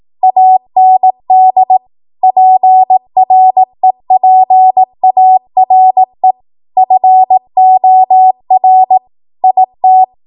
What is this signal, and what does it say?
Morse code, heard as a keyed tone, about 18 wpm, 750 Hz. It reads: AND PREPARE FOR IT
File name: Morse / 18WPM